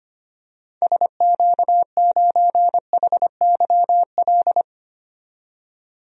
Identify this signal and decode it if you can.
Morse code, heard as a keyed tone, about 25 wpm, 700 Hz. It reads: SQ9HYL
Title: DXCC-100 Call Signs - 6 Characters - 25wpm